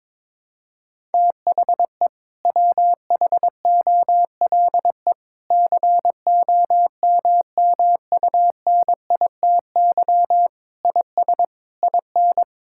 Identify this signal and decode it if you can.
Morse code, heard as a keyed tone, about 22 wpm, 700 Hz. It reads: THE WHOLE COMMUNITY IS IN
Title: Sentences from Top 1600 Words 22wpm